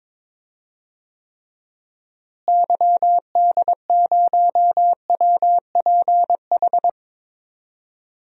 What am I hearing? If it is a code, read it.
Morse code, heard as a keyed tone, about 21 wpm, 700 Hz. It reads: YD0WPH
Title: DXCC-100 Call Signs - 6 Characters - Repeated 3x - 22wpm